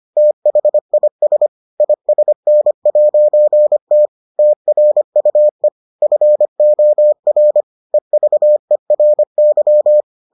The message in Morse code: THIS ISN'T TRUE FOR EVERY

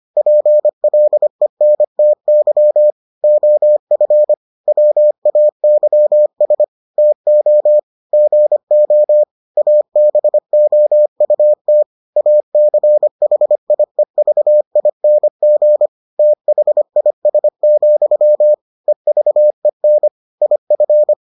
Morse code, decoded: PLENTY OF WAYS TO GO ABOUT ACHIEVING THIS, EVEN IF

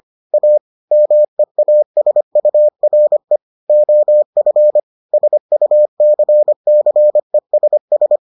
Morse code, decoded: A MEASURE OF SUCCESS

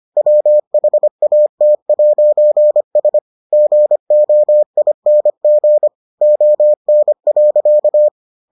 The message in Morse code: WHAT'S GOING ON.

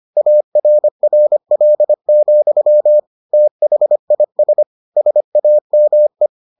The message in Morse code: ARRL, THIS SAME

